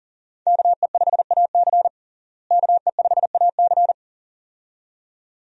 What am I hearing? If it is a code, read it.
Morse code, heard as a keyed tone, about 40 wpm, 700 Hz. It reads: KE5AC KE5AC
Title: USA Call Signs - 5 Characters - Repeated 2x - 40wpm